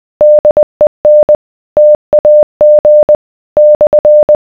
Morse code, heard as a keyed tone, about 20 wpm, 600 Hz. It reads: DEN TAG /